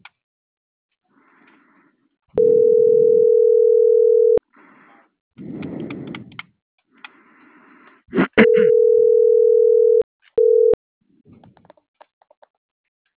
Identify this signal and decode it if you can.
Ringback tone (North America), 440 + 480 Hz.